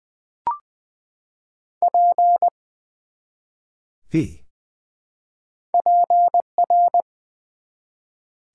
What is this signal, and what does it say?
Morse code, heard as a keyed tone, about 20 wpm, 700 Hz. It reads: P PR
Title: Build A Word - 1 Letter At A Time - 5-Letter Words - 20wpm